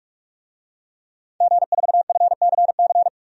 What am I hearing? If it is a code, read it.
Morse code, heard as a keyed tone, about 46 wpm, 700 Hz. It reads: G4FCC
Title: DXCC-100 Call Signs - 5 Characters - 45wpm